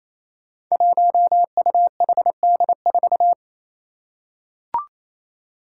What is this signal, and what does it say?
Morse code, heard as a keyed tone, about 27 wpm, 700 Hz. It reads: 1UHD4
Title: Five Challenging Letter-Number Combinations 28wpm